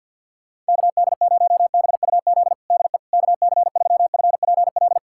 Morse code, heard as a keyed tone, about 51 wpm, 700 Hz. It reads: KD0BUZ DE KC3FPL
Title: USA Callsign DE Callsign - 6 Characters - 50wpm